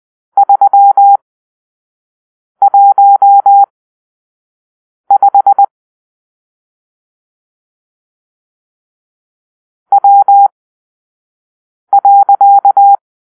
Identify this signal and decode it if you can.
Morse code, heard as a keyed tone, about 20 wpm, spaced as for 5 wpm, 800 Hz. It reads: 315 W.
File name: arrl / Grp-6 Drill-3